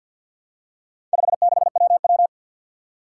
Morse code, heard as a keyed tone, about 51 wpm, 700 Hz. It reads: 56WW
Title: Four Challenging Letter-Number Combinations 50wpm